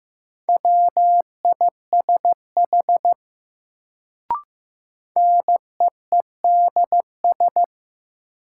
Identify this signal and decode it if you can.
Morse code, heard as a keyed tone, about 15 wpm, 700 Hz. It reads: WISH NEEDS
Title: Sentences from Top 1600 Words - New Word Drill Alt - 15wpm